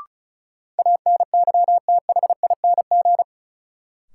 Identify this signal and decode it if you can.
Morse code, heard as a keyed tone, about 36 wpm, 700 Hz. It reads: ANYTHING
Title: Sentences from Top 1000 Words - New Word Drill - 35wpm